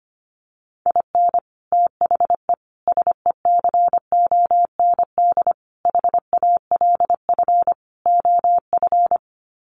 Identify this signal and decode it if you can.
Morse code, heard as a keyed tone, about 25 wpm, 700 Hz. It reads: IN THE SECOND HALF OF
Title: Sets of 5 Words 25wpm